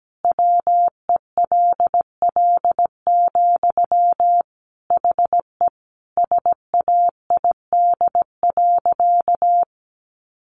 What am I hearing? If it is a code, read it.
Morse code, heard as a keyed tone, about 17 wpm, 700 Hz. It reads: WELL, HE SAID.
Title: Sentences from Top 700 Words 17wpm